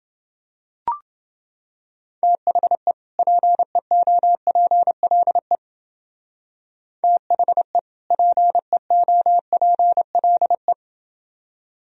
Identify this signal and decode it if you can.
Morse code, heard as a keyed tone, about 28 wpm, 700 Hz. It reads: THE PEOPLE THE PEOPLE